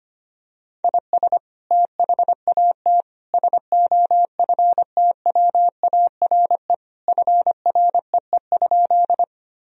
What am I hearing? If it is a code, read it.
Morse code, heard as a keyed tone, about 25 wpm, 700 Hz. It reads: IS THAT SOFTWARE FREE?
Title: Sentences from Top 1000 Words - Review Part 3 - 25wpm